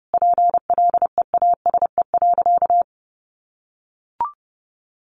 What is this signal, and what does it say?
Morse code, heard as a keyed tone, about 30 wpm, 700 Hz. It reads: PLEASE.
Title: Sentences from Top 1500 Words 30wpm